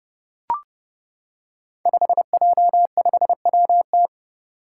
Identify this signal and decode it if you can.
Morse code, heard as a keyed tone, about 30 wpm, 700 Hz. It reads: 5J5WT
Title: Five Challenging Letter-Number Combinations 30wpm